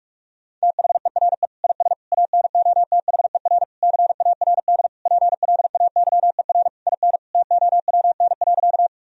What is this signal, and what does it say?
Morse code, heard as a keyed tone, about 41 wpm, 700 Hz. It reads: THERE IS ANOTHER CARD PLAYER IN TOWN.
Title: Sentences from Top 1000 Words - Review Part 4 - 45wpm